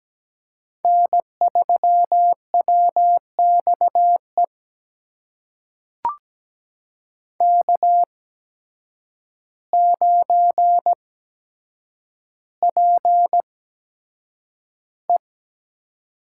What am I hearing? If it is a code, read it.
Morse code, heard as a keyed tone, about 17 wpm, spaced as for 8 wpm, 700 Hz. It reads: N3WXE K9PE
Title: USA Call Signs - 5 Characters - 8x Character Spacing - 17wpm